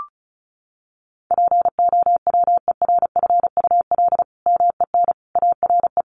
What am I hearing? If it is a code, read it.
Morse code, heard as a keyed tone, about 34 wpm, 700 Hz. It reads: POWERFUL MEN ARE
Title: Sentences from Top 1400 Words 35wpm